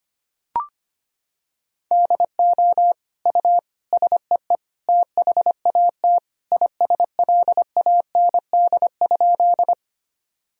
Morse code, heard as a keyed tone, about 25 wpm, 700 Hz. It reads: DO U SEE THAT ISLAND?
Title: Sentences from Top 500 Words and Abbreviations 25wpm